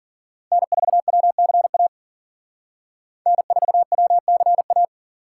Morse code, heard as a keyed tone, about 52 wpm, 700 Hz. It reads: N4WCA N4WCA